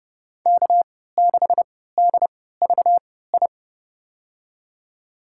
Morse code, heard as a keyed tone, about 30 wpm, spaced as for 17 wpm, 700 Hz. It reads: K6DVI
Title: USA Call Signs - 5 Characters - 3x Character Spacing - 30wpm